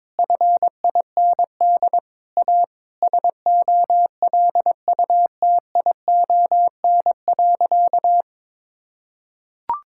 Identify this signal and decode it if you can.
Morse code, heard as a keyed tone, about 22 wpm, 700 Hz. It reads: FIND A SOLUTION.
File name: practice / 22wpm / Sentences with 8-Word Phrases - 22wpm